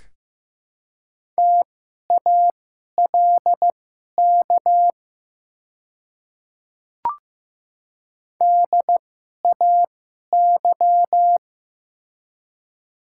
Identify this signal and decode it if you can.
Morse code, heard as a keyed tone, about 15 wpm, spaced as for 11 wpm, 700 Hz. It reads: TALK DAY
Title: 2x Character Spacing - Top 500 Words - 15wpm